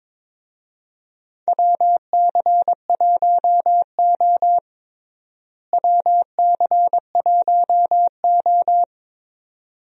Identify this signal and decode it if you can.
Morse code, heard as a keyed tone, about 22 wpm, 700 Hz. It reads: WC1O WC1O
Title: USA Call Signs - 4 Characters - Repeated 2x - 22wpm